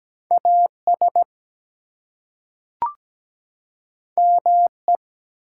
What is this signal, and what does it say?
Morse code, heard as a keyed tone, about 17 wpm, 700 Hz. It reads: AS ME
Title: Two Letter Words - 17wpm